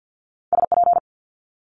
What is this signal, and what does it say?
Morse code, heard as a keyed tone, about 50 wpm, 700 Hz. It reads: SP
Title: Multi-letter Phonemes Quick Fire 50wpm